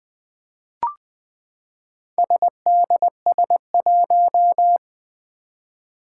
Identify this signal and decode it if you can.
Morse code, heard as a keyed tone, about 21 wpm, 700 Hz. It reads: SDS1
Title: Four Challenging Letter-Number Combinations 20wpm